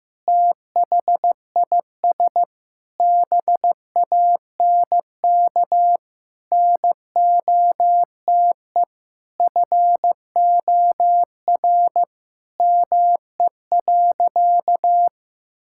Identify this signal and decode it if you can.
Morse code, heard as a keyed tone, about 15 wpm, 700 Hz. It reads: THIS BANK NOTE FOR ME.
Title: Sentences from Top 1000 Words - Review Part 6 - 15wpm